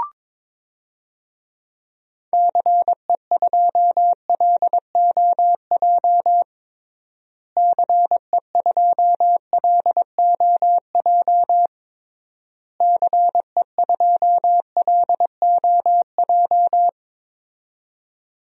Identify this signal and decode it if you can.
Morse code, heard as a keyed tone, about 22 wpm, 700 Hz. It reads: CE2LOJ CE2LOJ CE2LOJ
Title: DXCC-100 Call Signs - 6 Characters - Repeated 3x - 22wpm